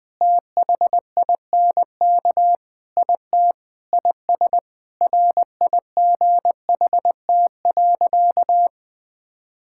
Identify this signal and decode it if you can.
Morse code, heard as a keyed tone, about 20 wpm, 700 Hz. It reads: THINK IT IS RIGHT.
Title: Sentences with 5-Word Phrases - 20wpm